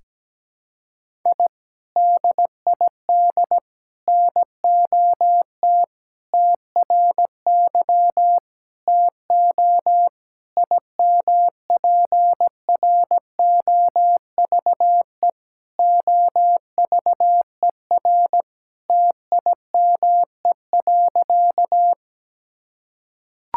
Morse code, heard as a keyed tone, about 17 wpm, 700 Hz. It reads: I DID NOT TRY TO IMPROVE OVER TIME.
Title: Sentences from Top 1000 Words - Review Part 5 - 17wpm